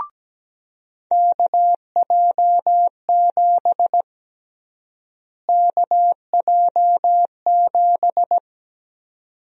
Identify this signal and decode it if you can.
Morse code, heard as a keyed tone, about 17 wpm, 700 Hz. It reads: KJ7 KJ7
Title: USA Call Sign Prefixes - Repeated 2x - 17wpm